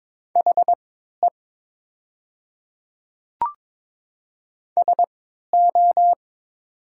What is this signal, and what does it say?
Morse code, heard as a keyed tone, about 22 wpm, spaced as for 12 wpm, 700 Hz. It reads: HE SO
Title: 3x Character Spacing - Top 100 Words - 22wpm